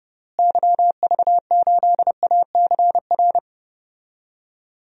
Morse code, heard as a keyed tone, about 30 wpm, 700 Hz. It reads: YV8ACR
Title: DXCC-100 Call Signs - 6 Characters - Repeated 3x - 30wpm